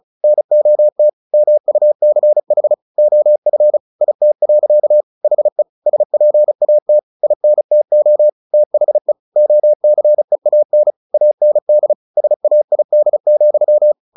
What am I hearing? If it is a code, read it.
Morse code, heard as a keyed tone, about 37 wpm, 600 Hz. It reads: NOT MUCH OF IT. HE SPAT INTO THE OCEAN AND SAID,
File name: the old man and the sea 2 35wpm